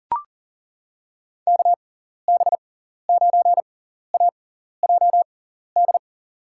Morse code, heard as a keyed tone, about 40 wpm, spaced as for 14 wpm, 700 Hz. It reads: KB9AJD
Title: USA Call Signs - 6 Characters - 6x Character Spacing - 40wpm